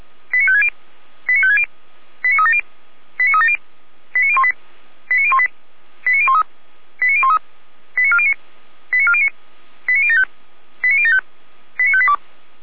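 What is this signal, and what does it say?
Five-tone selective call (ZVEI), heard as quick runs of five tones, tones of about 70 ms. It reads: 89560 89560 89370 89370 89018 89018 89013 89013 89409 89409 89075 89075 89582
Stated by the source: ZVEI five-tone selective call received on an SDR, recorded by Feuerwehr Orlamünde (f85360).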